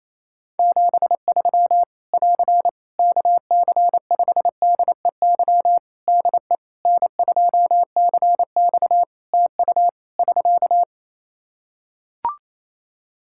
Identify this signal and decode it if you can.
Morse code, heard as a keyed tone, about 28 wpm, 700 Hz. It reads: 73 <AR> KC5DEY DE N2CX TU <SK>
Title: Practice QSO - Exchange 4 - Ending - 28wpm